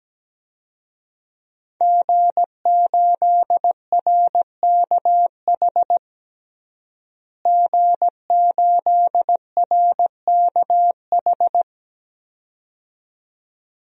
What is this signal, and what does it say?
Morse code, heard as a keyed tone, about 17 wpm, 700 Hz. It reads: G8RKH G8RKH
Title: DXCC-100 Call Signs - 5 Characters - Repeated 2x - 17wpm